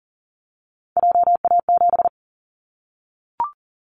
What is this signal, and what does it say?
Morse code, heard as a keyed tone, about 40 wpm, 700 Hz. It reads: JA7
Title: Callsign Prefixes for Top 50 DXCC - 40wpm